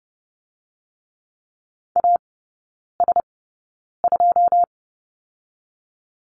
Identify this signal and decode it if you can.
Morse code, heard as a keyed tone, about 30 wpm, 700 Hz. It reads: A S 2
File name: Lesson 19e - Letter w - Three Characters 30wpm